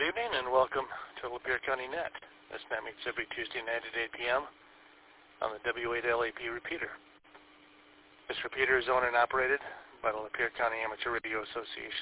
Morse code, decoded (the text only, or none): none